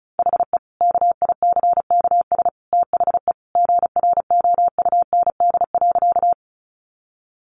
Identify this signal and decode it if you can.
Morse code, heard as a keyed tone, about 34 wpm, 700 Hz. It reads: HE KICKS THE GROUND.
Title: Additional Easy Phrases 35wpm